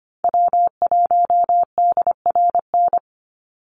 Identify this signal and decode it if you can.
Morse code, heard as a keyed tone, about 25 wpm, 700 Hz. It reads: W1DRN
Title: USA Call Signs 25wpm